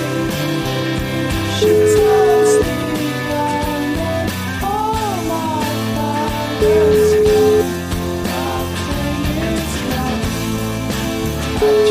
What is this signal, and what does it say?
Ringback tone (Europe, 425 Hz), with noise.